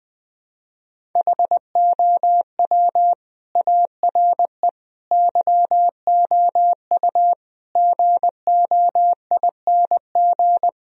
Morse code, heard as a keyed tone, about 20 wpm, 700 Hz. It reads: HOW ARE YOU GOING
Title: Speed-Racing - Sets of 5 Words 20wpm